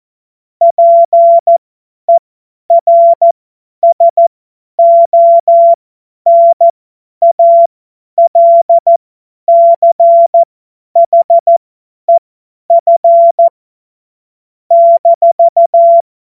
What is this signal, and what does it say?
Morse code, heard as a keyed tone, about 14 wpm, spaced as for 10 wpm, 680 Hz. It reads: PERSONALCHEF -
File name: Morsekurs / Witze / Witz 10 Juni 2025 50 F70